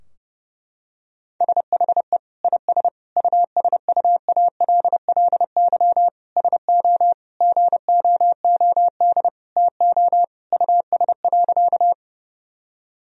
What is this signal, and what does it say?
Morse code, heard as a keyed tone, about 30 wpm, 700 Hz. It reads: SHE IS USUALLY SO GOOD TO US.
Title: Sentences from Top 1000 Words - Review Part 3 - 30wpm